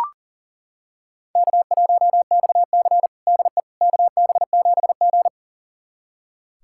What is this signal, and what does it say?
Morse code, heard as a keyed tone, about 40 wpm, 700 Hz. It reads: K1XC DE KB7G
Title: USA Callsign DE Callsign - 4 Characters - 40wpm